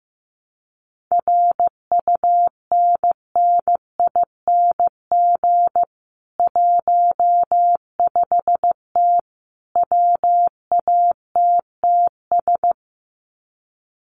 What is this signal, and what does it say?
Morse code, heard as a keyed tone, about 15 wpm, 700 Hz. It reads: RUNNING 15T WATTS
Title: Common QSO Phrases 15wpm